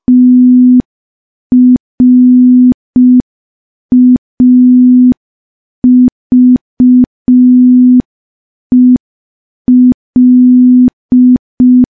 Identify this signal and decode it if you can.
Morse code, heard as a keyed tone, about 5 wpm, 250 Hz. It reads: TRAVEL
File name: morse / morse-training-05WPM